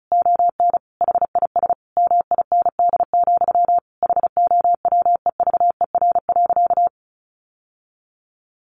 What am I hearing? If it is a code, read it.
Morse code, heard as a keyed tone, about 34 wpm, 700 Hz. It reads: ON HIS MIND, HOWEVER.
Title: Sentences from Top 1100 Words 35wpm